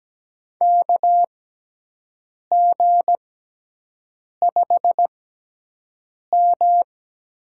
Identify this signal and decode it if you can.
Morse code, heard as a keyed tone, about 17 wpm, spaced as for 6 wpm, 700 Hz. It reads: KG5M
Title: USA Call Signs - 6 Characters - 6x Character Spacing - 17wpm